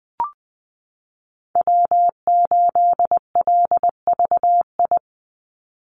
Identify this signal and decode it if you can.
Morse code, heard as a keyed tone, about 20 wpm, 700 Hz. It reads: W8LVI
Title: Call Signs Repeated 2x - 20wpm